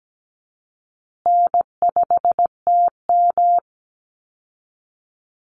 Morse code, heard as a keyed tone, about 17 wpm, 700 Hz. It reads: N5TM
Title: Easy Call Signs Repeated 3x - 17wpm